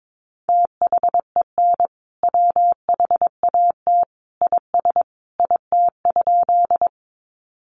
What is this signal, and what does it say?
Morse code, heard as a keyed tone, about 22 wpm, 700 Hz. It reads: THEN WHAT IS IT?
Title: Sentences from Top 100 Words 22wpm